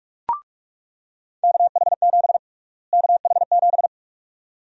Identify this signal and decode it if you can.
Morse code, heard as a keyed tone, about 46 wpm, 700 Hz. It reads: KH7 KH7
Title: USA Call Sign Prefixes - Repeated 2x - 45wpm